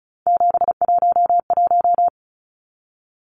Morse code, heard as a keyed tone, about 34 wpm, 700 Hz. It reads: 711
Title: Numbers - 3-Digit 35wpm